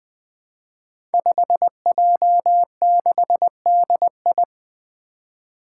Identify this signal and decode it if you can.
Morse code, heard as a keyed tone, about 20 wpm, 700 Hz. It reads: 5J6DI
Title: Five Challenging Letter-Number Combinations 20wpm